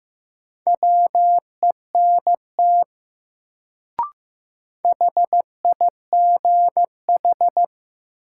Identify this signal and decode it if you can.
Morse code, heard as a keyed tone, about 15 wpm, 700 Hz. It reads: WENT HIGH
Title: Sentences from Top 200 Words - New Word Drill - 15wpm